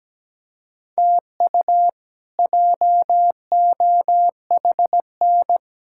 Morse code, heard as a keyed tone, about 17 wpm, 700 Hz. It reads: TU JOHN
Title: K1USN SST QSO Practice - 17wpm